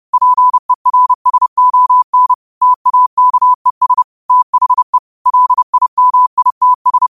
Morse code, heard as a keyed tone, about 30 wpm, 1000 Hz. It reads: PERSON TAKES THE LIMITS
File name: qotdfiles / 30wpm